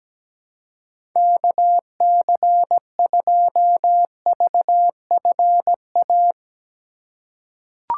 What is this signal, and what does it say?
Morse code, heard as a keyed tone, about 17 wpm, 700 Hz. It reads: KC2VFA
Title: USA Call Signs - 6 Characters - 4x Character Spacing - 17wpm